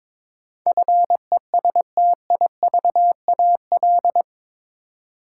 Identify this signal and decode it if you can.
Morse code, heard as a keyed tone, about 22 wpm, 700 Hz. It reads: FESTIVAL
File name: practice / 22wpm / Eight Letter Words - 22wpm